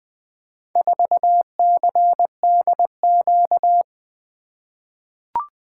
Morse code, heard as a keyed tone, about 20 wpm, 700 Hz. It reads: 4CDQ
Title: Four Letter-Number Combinations - 20wpm